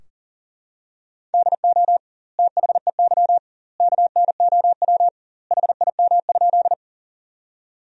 Morse code, heard as a keyed tone, about 37 wpm, 700 Hz. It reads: DO THEY KNOW HIM?
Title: 2x Word-Spacing - Sentences from Top 100 Words - 40wpm